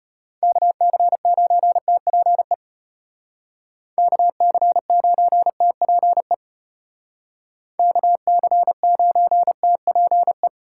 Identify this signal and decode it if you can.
Morse code, heard as a keyed tone, about 33 wpm, 700 Hz. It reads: KC9TPE KC9TPE KC9TPE